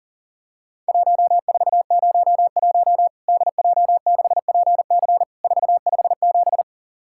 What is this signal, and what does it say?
Morse code, heard as a keyed tone, about 40 wpm, 700 Hz. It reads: 1401 DJ6PC 457
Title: Worked All Europe DX Contest - QTC Exchange - 40wpm